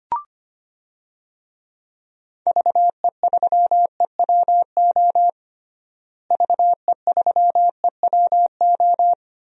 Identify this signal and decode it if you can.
Morse code, heard as a keyed tone, about 25 wpm, 700 Hz. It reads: VE3EWO VE3EWO
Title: DXCC-100 Call Signs - 6 Characters - Repeated 3x - 25wpm